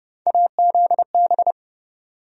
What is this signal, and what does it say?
Morse code, heard as a keyed tone, about 32 wpm, 700 Hz. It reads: AZB
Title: USA Call Sign Postfixes - 3 Letters 30wpm